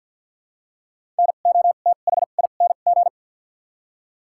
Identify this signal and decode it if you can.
Morse code, heard as a keyed tone, about 53 wpm, 700 Hz. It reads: N O T H I N G
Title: 2x Character Spacing - Top 500 Words - 50wpm